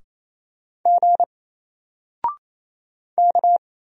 Morse code, heard as a keyed tone, about 27 wpm, 700 Hz. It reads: G K
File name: Single-Letter-Number-28wpm